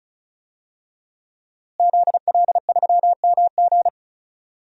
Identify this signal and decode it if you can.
Morse code, heard as a keyed tone, about 36 wpm, 700 Hz. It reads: ZL3MG
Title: DXCC-100 Call Signs - 5 Characters - Repeated 2x - 35wpm